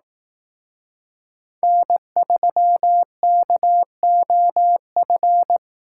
Morse code, heard as a keyed tone, about 18 wpm, 700 Hz. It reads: N3KOF
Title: Speed Racing - USA 1x3 Call Signs 15wpm